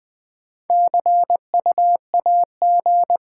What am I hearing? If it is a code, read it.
Morse code, heard as a keyed tone, about 20 wpm, 690 Hz. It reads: CU AG
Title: Top 20 QSO Elements - 6x Character Spacing - 20wpm